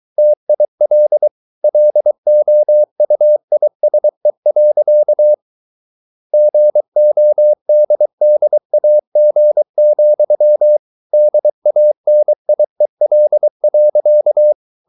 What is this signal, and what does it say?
Morse code, heard as a keyed tone, about 23 wpm, 600 Hz. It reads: TIL LOUISE. GODDAG, DANIEL.
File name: Ferie QRS 23wpm